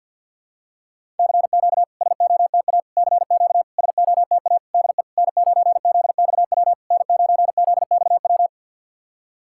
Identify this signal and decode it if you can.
Morse code, heard as a keyed tone, about 53 wpm, 690 Hz. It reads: CQ SOTA CQ SOTA DE N9ZXW N9ZXW
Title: SOTA QSO Practice 50wpm